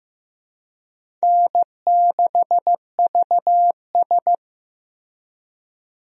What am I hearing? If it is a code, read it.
Morse code, heard as a keyed tone, about 15 wpm, 700 Hz. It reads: N6VS
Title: USA Call Signs - 4 Characters - Repeated 3x - 15wpm